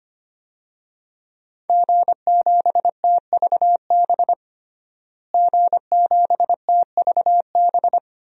Morse code, heard as a keyed tone, about 25 wpm, 700 Hz. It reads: G7TVB G7TVB
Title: DXCC-100 Call Signs - 5 Characters - Repeated 3x - 25wpm